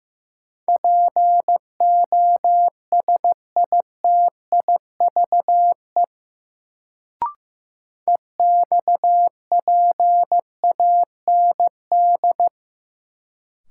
Morse code, heard as a keyed tone, about 15 wpm, 700 Hz. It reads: POSITIVE EXPAND
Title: Sentences from Top 1400 Words - New Word Drill Alt - 15wpm